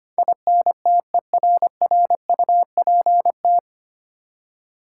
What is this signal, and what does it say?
Morse code, heard as a keyed tone, about 25 wpm, 700 Hz. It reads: INTERRUPT
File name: Nine Letter Words - 25wpm